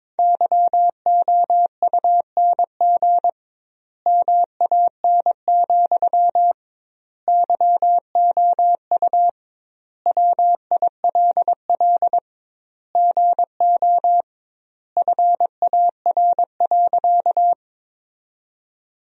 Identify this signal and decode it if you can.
Morse code, heard as a keyed tone, about 21 wpm, 700 Hz. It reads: YOUNG MAN, YOU WILL GO FAR.
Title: 2x Word-Spacing - Sentences from Top 400 Words - 22wpm